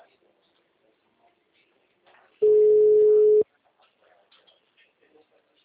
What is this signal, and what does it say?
Ringback tone (Europe, 425 Hz), with speech.